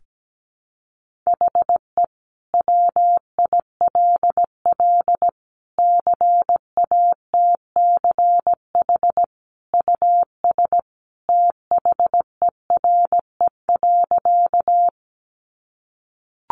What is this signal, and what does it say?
Morse code, heard as a keyed tone, about 17 wpm, 700 Hz. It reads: HE WILL CATCH US THERE.
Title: Sentences from Top 1500 Words 17wpm